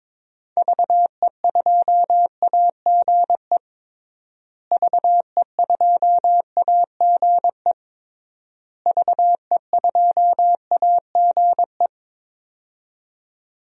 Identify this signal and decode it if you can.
Morse code, heard as a keyed tone, about 22 wpm, 700 Hz. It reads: VE2AGE VE2AGE VE2AGE
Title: DXCC-100 Call Signs - 6 Characters - Repeated 3x - 22wpm